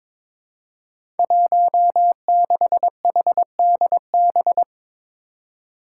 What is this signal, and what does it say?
Morse code, heard as a keyed tone, about 21 wpm, 700 Hz. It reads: 16HDB
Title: Five Challenging Letter-Number Combinations 22wpm